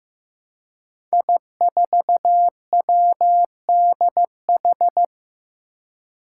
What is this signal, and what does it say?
Morse code, heard as a keyed tone, about 15 wpm, 700 Hz. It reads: I4WDH
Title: Five Challenging Letter-Number Combinations 15wpm